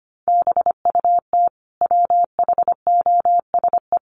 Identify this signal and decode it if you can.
Morse code, heard as a keyed tone, about 25 wpm, 700 Hz. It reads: BUT WHOSE
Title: Sentences from Top 1500 Words 25wpm